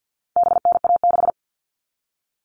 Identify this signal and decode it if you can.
Morse code, heard as a keyed tone, about 50 wpm, 700 Hz. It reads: BNA6
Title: All Cut Numbers - 4-Digit 50wpm